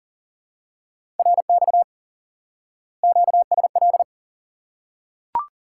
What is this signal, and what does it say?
Morse code, heard as a keyed tone, about 40 wpm, 700 Hz. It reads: RX QSL
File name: IQE Course 3 - Lesson 89 - YAGI - Two Elements - 40wpm